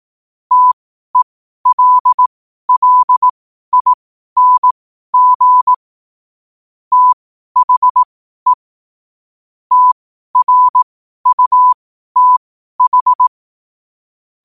Morse code, heard as a keyed tone, about 18 wpm, spaced as for 12 wpm, 1000 Hz. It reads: TELLING THE TRUTH
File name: qotdfiles / 12wpm